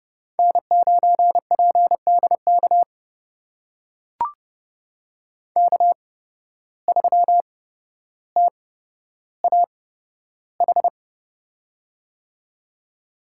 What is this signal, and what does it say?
Morse code, heard as a keyed tone, about 30 wpm, spaced as for 13 wpm, 700 Hz. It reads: N9PDK K3TAH
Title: USA Call Signs - 5 Characters - 8x Character Spacing - 30wpm